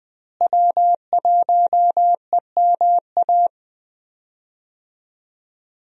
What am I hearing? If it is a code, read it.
Morse code, heard as a keyed tone, about 21 wpm, 700 Hz. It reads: W1EMA
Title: USA Call Signs - 5 Characters - Repeated 3x - 20wpm